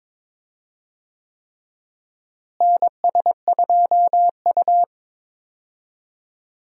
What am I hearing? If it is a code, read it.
Morse code, heard as a keyed tone, about 22 wpm, 700 Hz. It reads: NS2U
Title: USA Call Signs - 4 Characters - Repeated 3x - 22wpm